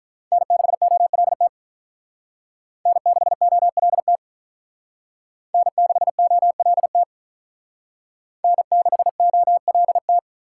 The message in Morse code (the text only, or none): N6OLT N6OLT N6OLT N6OLT